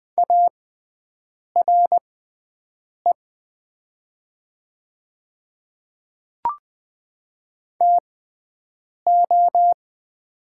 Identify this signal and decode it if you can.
Morse code, heard as a keyed tone, about 20 wpm, spaced as for 7 wpm, 700 Hz. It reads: ARE TO
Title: 6x Character Spacing - Top 100 Words - 20wpm